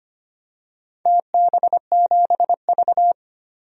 Morse code, heard as a keyed tone, about 25 wpm, 700 Hz. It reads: TB7V
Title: DXCC-100 Call Signs - 4 Characters - 25wpm